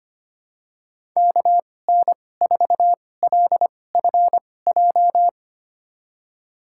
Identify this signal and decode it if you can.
Morse code, heard as a keyed tone, about 25 wpm, spaced as for 18 wpm, 700 Hz. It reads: KN4LFJ
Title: USA Call Signs - 6 Characters - 2x Character Spacing - 25wpm